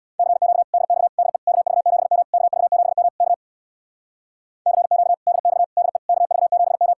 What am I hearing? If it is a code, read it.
Morse code, heard as a keyed tone, about 62 wpm, 700 Hz. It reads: CQ DX DE KV7M KV7M K CQ DX DE KV7M
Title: Speed-Racing Common QSO Phrases Encore - 50wpm